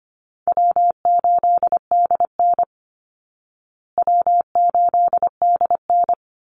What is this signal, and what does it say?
Morse code, heard as a keyed tone, about 25 wpm, 700 Hz. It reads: W8DN W8DN
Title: Call Signs Repeated 2x - 25wpm